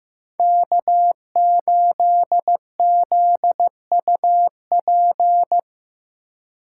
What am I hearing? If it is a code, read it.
Morse code, heard as a keyed tone, about 18 wpm, 700 Hz. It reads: K8ZUP
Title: Speed Racing - USA Call Signs 15wpm